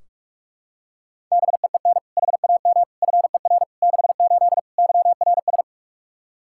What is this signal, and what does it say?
Morse code, heard as a keyed tone, about 46 wpm, 700 Hz. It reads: BEEN HAM FER 68 YRS
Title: K1USN SST QSO Practice 45wpm